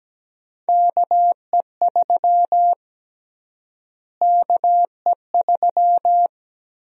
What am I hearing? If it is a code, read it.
Morse code, heard as a keyed tone, about 17 wpm, 700 Hz. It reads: KE3 KE3
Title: USA Call Sign Prefixes - Repeated 2x - 17wpm